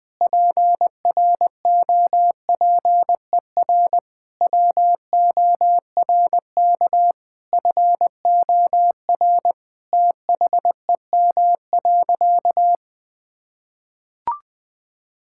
Morse code, heard as a keyed tone, about 20 wpm, 700 Hz. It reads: PROPER WORK FOR THEM.
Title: Sentences from Top 1000 Words - Review Part 5 - 20wpm